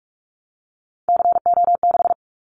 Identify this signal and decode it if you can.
Morse code, heard as a keyed tone, about 44 wpm, 700 Hz. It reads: CO6
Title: Callsign Prefixes for Top 100 DXCC - 45wpm